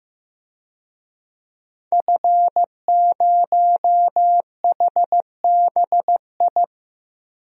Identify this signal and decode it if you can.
Morse code, heard as a keyed tone, about 15 wpm, 700 Hz. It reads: F0HBI
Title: DXCC-100 Call Signs - 5 Characters - 15wpm